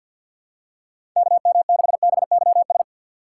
Morse code, heard as a keyed tone, about 47 wpm, 700 Hz. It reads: KM6BYS
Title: USA Call Signs - 3x Character Spacing - 50wpm